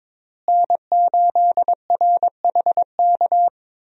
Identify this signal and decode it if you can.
Morse code, heard as a keyed tone, about 21 wpm, 700 Hz. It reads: N8RHK
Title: USA Call Signs - 5 Characters - Repeated 3x - 22wpm